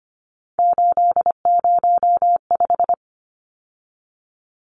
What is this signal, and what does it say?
Morse code, heard as a keyed tone, about 25 wpm, 700 Hz. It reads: 805
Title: Numbers - 3-Digit 25wpm